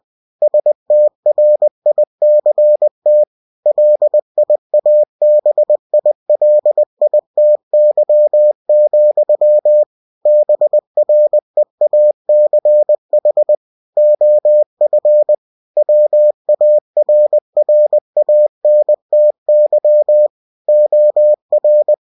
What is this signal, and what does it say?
Morse code, heard as a keyed tone, about 20 wpm, 600 Hz. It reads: STRICT LIABILITY, BREACH OF WARRANTY OR